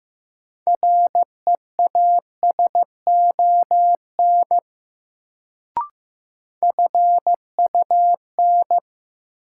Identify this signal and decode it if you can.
Morse code, heard as a keyed tone, about 16 wpm, 700 Hz. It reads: REASON FUN
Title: Sentences from Top 700 Words - New Word Drill - 15wpm